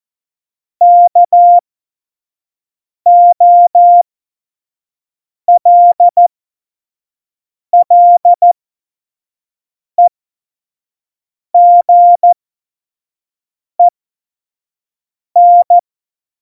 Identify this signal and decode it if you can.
Morse code, heard as a keyed tone, about 14 wpm, spaced as for 5 wpm, 700 Hz. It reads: KOLLEGEN
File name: Morsekurs / Witze / Witz 19. Januar 2021, 25 F70